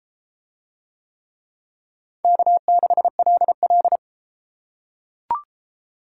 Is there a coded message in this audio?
Yes: Morse code from 2.2 to 4.0 s, heard as a keyed tone, about 33 wpm, 700 Hz. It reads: K6LL